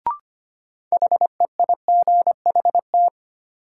Morse code, heard as a keyed tone, about 25 wpm, 700 Hz. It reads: HEIGHT
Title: Sentences from Top 1600 Words - New Word Drill Alt - 25wpm